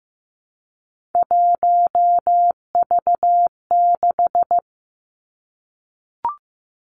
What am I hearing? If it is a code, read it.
Morse code, heard as a keyed tone, about 15 wpm, 700 Hz. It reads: 1V6
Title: Three Challenging Letter-Number Combinations 15wpm